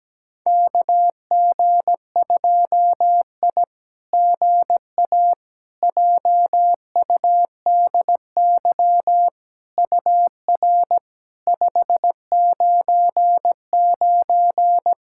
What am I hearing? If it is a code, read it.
Morse code, heard as a keyed tone, about 17 wpm, 700 Hz. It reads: KG2I GA JUDY UR 599
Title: SOTA QSO Practice 17wpm